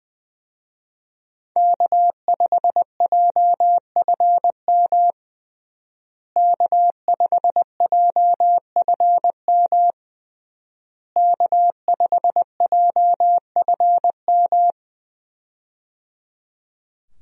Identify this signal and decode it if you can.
Morse code, heard as a keyed tone, about 20 wpm, 700 Hz. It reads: K5JFM K5JFM K5JFM
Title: USA Call Signs - 5 Characters - Repeated 3x - 20wpm